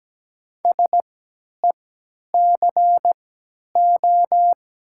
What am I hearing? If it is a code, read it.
Morse code, heard as a keyed tone, about 17 wpm, spaced as for 10 wpm, 700 Hz. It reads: SECO
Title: 3x Character Spacing - Top 300 Words - 17wpm